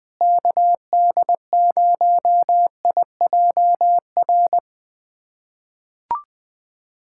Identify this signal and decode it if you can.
Morse code, heard as a keyed tone, about 20 wpm, 700 Hz. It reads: KD0IJR
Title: USA Callsign DE Callsign - 6 Characters - 20wpm